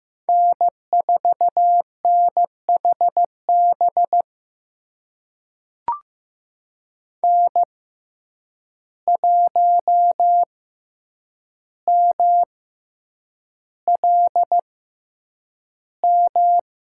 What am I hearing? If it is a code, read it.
Morse code, heard as a keyed tone, about 15 wpm, spaced as for 8 wpm, 700 Hz. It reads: N4NHB N1MLM
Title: USA Call Signs - 6x Character Spacing - 15wpm